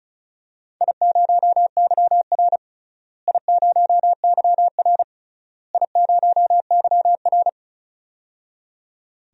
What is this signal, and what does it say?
Morse code, heard as a keyed tone, about 36 wpm, 700 Hz. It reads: I0YR I0YR I0YR
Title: DXCC-100 Call Signs - 4 Characters - Repeated 3x - 35wpm